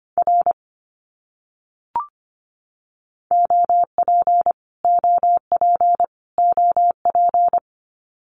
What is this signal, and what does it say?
Morse code, heard as a keyed tone, about 25 wpm, 700 Hz. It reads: R OP OP OP
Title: Abbreviations Repeated 3x - 25wpm